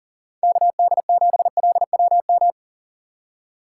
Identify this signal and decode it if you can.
Morse code, heard as a keyed tone, about 50 wpm, 700 Hz. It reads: KD7LWM
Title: Speed Racing - USA 2x3 Call Signs 40wpm